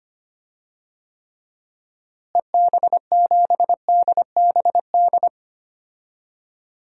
Morse code, heard as a keyed tone, about 25 wpm, 700 Hz. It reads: EB7DBD
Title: DXCC-100 Call Signs - 6 Characters - Repeated 3x - 25wpm